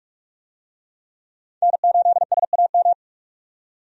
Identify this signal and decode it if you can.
Morse code, heard as a keyed tone, about 46 wpm, 700 Hz. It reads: N8SAM
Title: USA Call Signs - 5 Characters - 45wpm